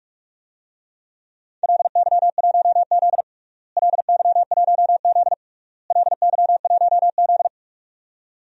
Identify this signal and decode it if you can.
Morse code, heard as a keyed tone, about 42 wpm, 700 Hz. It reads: LY1Z LY1Z LY1Z
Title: DXCC-100 Call Signs - 4 Characters - Repeated 3x - 45wpm